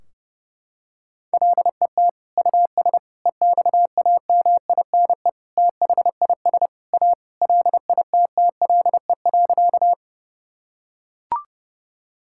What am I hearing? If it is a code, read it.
Morse code, heard as a keyed tone, about 32 wpm, 700 Hz. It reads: LET US EXAMINE THIS A LITTLE.
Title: Sentences from Top 1600 Words 30wpm